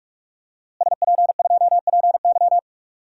Morse code, heard as a keyed tone, about 46 wpm, 700 Hz. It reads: SP2PY
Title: DXCC-100 Call Signs - 5 Characters - Repeated 2x - 45wpm